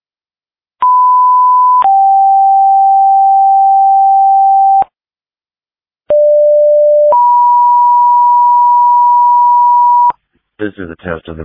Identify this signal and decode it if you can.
Two-tone page, heard as a one-second tone, then a three-second tone, about 1000 then 780 Hz, about 590 then 970 Hz, with speech. It is heard from 0.8 to 10.1 s.
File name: dispatch / Tripoli Fire 2026 07 09 19 06 59